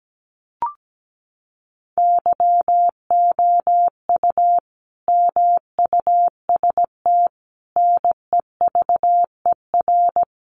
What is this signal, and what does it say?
Morse code, heard as a keyed tone, about 17 wpm, 700 Hz. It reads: YOU MUST NEVER
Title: Sentences from Top 900 Words 17wpm